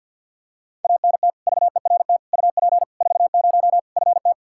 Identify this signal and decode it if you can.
Morse code, heard as a keyed tone, about 46 wpm, 700 Hz. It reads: ANT VERT UP 40 FT
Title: Practice QSO - Exchange 2 - Info - 50wpm